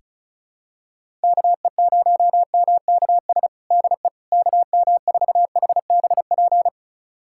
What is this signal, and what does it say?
Morse code, heard as a keyed tone, about 34 wpm, 700 Hz. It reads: KE0MKS DE KM4HBP
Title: USA Callsign DE Callsign - 6 Characters - 35wpm